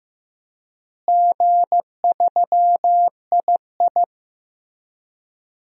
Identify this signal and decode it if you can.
Morse code, heard as a keyed tone, about 15 wpm, 700 Hz. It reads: G3II
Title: DXCC-100 Call Signs - 4 Characters - Repeated 3x - 15wpm